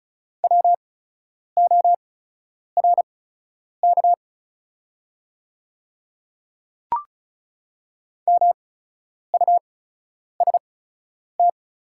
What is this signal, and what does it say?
Morse code, heard as a keyed tone, about 36 wpm, spaced as for 10 wpm, 700 Hz. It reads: WORK MUST
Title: 8x Character Spacing - Top 200 Words - 35wpm